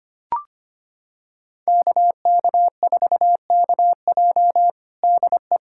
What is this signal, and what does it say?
Morse code, heard as a keyed tone, about 25 wpm, 700 Hz. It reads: KK4KJ DE
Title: USA Callsign DE Callsign - 5 Characters - 25wpm